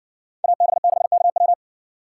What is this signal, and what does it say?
Morse code, heard as a keyed tone, about 60 wpm, 700 Hz. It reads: AB6KW